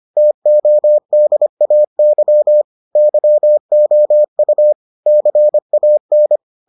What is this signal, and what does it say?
Morse code, heard as a keyed tone, about 25 wpm, 600 Hz. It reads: TODAY YOU CAN